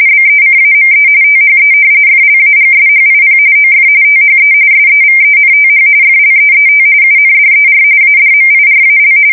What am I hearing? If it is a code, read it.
RTTY, about 45.45 baud. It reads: BECAME THE FLAGSHIP OF THE SEVENTH FLEET UNDER ADMIRAL
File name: fsk170 / battleshipnj